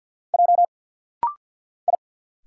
Morse code, heard as a keyed tone, about 50 wpm, 700 Hz. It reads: J I
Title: Single Letters - Rapid-Fire 50wpm